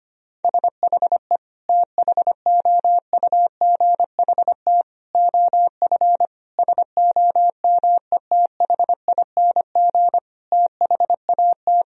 Morse code, heard as a keyed tone, about 25 wpm, 700 Hz. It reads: SHE THOUGHT OF SOMETHING THAT
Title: Sentences from Top 1000 Words - Review Part 1 - 25wpm